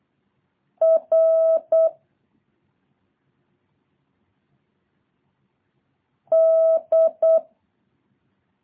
Morse code, heard as a keyed tone, about 8 wpm, 650 Hz. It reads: R D
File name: jh8cbh / 2023cw